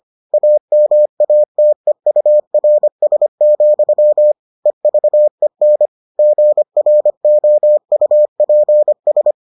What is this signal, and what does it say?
Morse code, heard as a keyed tone, about 25 wpm, 600 Hz. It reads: AMATEURS, EVEN GROUPS